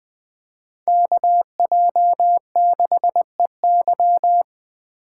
Morse code, heard as a keyed tone, about 20 wpm, 700 Hz. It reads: KJ6EY
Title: Speed Racing - USA Call Signs 17wpm